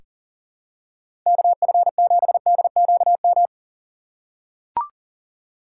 Morse code, heard as a keyed tone, about 40 wpm, 700 Hz. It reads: KF7DQM
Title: USA Call Signs - 6 Characters - 8x Character Spacing - 40wpm